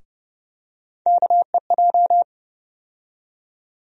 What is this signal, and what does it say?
Morse code, heard as a keyed tone, about 30 wpm, 700 Hz. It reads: KEJ
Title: USA Call Sign Postfixes - 3 Letters 30wpm